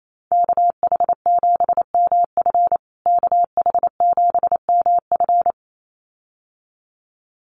Morse code, heard as a keyed tone, about 28 wpm, 700 Hz. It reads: KH7MF KH7MF
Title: USA Call Signs Repeated 2x 28wpm